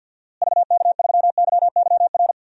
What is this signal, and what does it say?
Morse code, heard as a keyed tone, about 50 wpm, 700 Hz. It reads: VK3YYR
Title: DXCC-100 Call Signs - 6 Characters - Repeated 2x - 50wpm